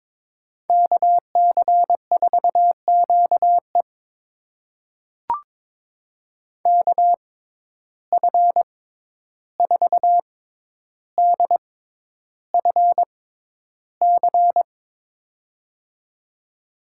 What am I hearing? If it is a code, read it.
Morse code, heard as a keyed tone, about 23 wpm, spaced as for 11 wpm, 700 Hz. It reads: KC4QE KF4DFC
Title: USA Call Signs - 6x Character Spacing - 22wpm